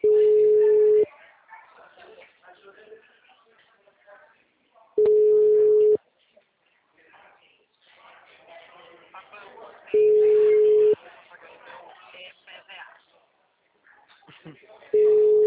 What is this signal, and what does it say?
Ringback tone (Europe, 425 Hz), with speech.